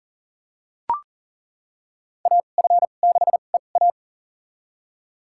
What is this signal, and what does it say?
Morse code, heard as a keyed tone, about 37 wpm, spaced as for 28 wpm, 700 Hz. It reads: AF6EA
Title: USA Call Signs - 5 Characters - 2x Character Spacing - 40wpm